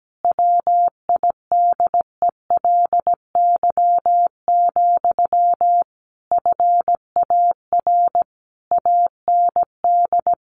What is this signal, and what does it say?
Morse code, heard as a keyed tone, about 17 wpm, 700 Hz. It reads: WIDELY, FAR AND
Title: Sentences from Top 1200 Words 17wpm